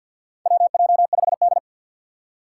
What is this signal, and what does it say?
Morse code, heard as a keyed tone, about 52 wpm, 700 Hz. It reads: WJ5D
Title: USA Call Signs - 2x Character Spacing - 50wpm